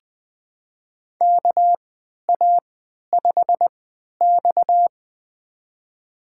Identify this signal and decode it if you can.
Morse code, heard as a keyed tone, about 20 wpm, spaced as for 11 wpm, 700 Hz. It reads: KA5X
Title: USA Call Signs - 4 Characters - 3x Character Spacing - 20wpm